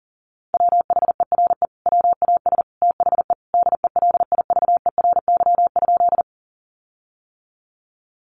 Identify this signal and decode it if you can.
Morse code, heard as a keyed tone, about 40 wpm, 700 Hz. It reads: WHERE WAS THE DELIVERY?
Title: Sentences from Top 1500 Words 40wpm